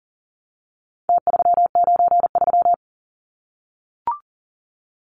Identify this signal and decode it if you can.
Morse code, heard as a keyed tone, about 40 wpm, 700 Hz. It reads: T393
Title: Numbers - Contest Style - 4-Digit 40wpm